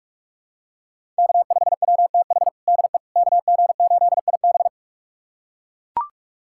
Morse code, heard as a keyed tone, about 43 wpm, 700 Hz. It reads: K5WTH DE KG8IB
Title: USA Callsign DE Callsign - 5 Characters - 45wpm